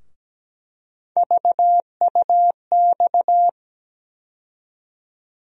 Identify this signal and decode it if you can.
Morse code, heard as a keyed tone, about 17 wpm, 700 Hz. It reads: VUX
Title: USA Call Sign Postfixes - 3 Letters 17wpm